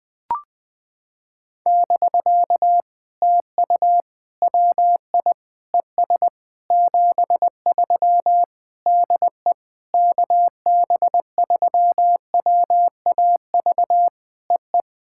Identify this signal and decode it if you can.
Morse code, heard as a keyed tone, about 20 wpm, 700 Hz. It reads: <BK> TU WI ES 73 DE KB3WAV EE
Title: POTA QSO Practice - 20wpm